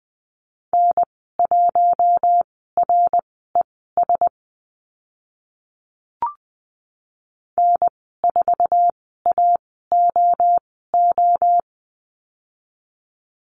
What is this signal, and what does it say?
Morse code, heard as a keyed tone, about 19 wpm, spaced as for 14 wpm, 700 Hz. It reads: N1RES N4AOO
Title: 055 Lesson 10g - Letter r - Call Signs 20wpm